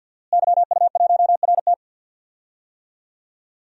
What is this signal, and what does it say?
Morse code, heard as a keyed tone, about 47 wpm, 700 Hz. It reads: YU1RT
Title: DXCC-100 Call Signs - 5 Characters - Repeated 3x - 50wpm